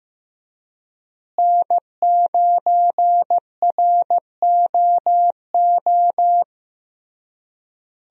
Morse code, heard as a keyed tone, about 15 wpm, 700 Hz. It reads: N9ROO
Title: USA Call Signs - 5 Characters - Repeated 3x - 15wpm